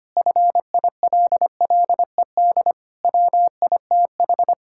Morse code, heard as a keyed tone, about 25 wpm, 700 Hz. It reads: FILLED WITH
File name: practice / 25wpm / Difficult Sentences with Hints - 25wpm